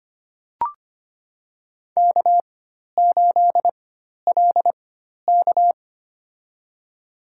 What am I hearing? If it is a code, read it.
Morse code, heard as a keyed tone, about 25 wpm, spaced as for 12 wpm, 700 Hz. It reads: K8LK
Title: USA Call Signs - 4 Characters - 4x Character Spacing - 25wpm